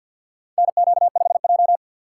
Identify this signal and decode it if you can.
Morse code, heard as a keyed tone, about 51 wpm, 700 Hz. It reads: NQ5J
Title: USA Call Signs - 3x Character Spacing - 50wpm